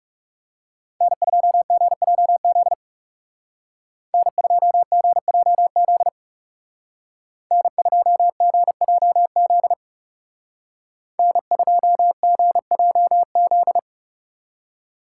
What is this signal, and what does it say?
Morse code, heard as a keyed tone, about 39 wpm, 700 Hz. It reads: N2GJZ N2GJZ N2GJZ N2GJZ